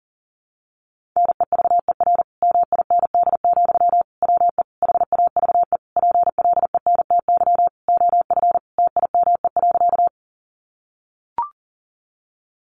Morse code, heard as a keyed tone, about 37 wpm, 700 Hz. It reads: NEVER MIND, WE HAVE PLENTY OF TIME.
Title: Sentences from Top 1300 Words 40wpm